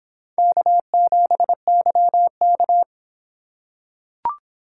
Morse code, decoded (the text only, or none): K7YK